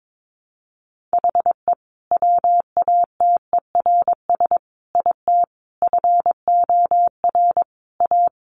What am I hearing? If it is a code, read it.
Morse code, heard as a keyed tone, about 22 wpm, 700 Hz. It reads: HE WATERS IT FOR A
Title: Easy Phrases 22wpm